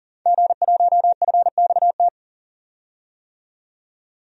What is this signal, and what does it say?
Morse code, heard as a keyed tone, about 39 wpm, 700 Hz. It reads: G1FXT
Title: DXCC-100 Call Signs - 5 Characters - 40wpm